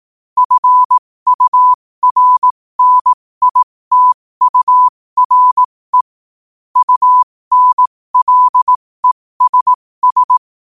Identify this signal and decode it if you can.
Morse code, heard as a keyed tone, about 18 wpm, spaced as for 15 wpm, 1000 Hz. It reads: FURNITURE UNLESS
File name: qotdfiles / 15wpm